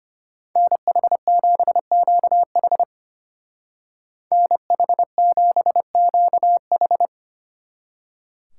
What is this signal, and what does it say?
Morse code, heard as a keyed tone, about 33 wpm, 700 Hz. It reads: NH7QH NH7QH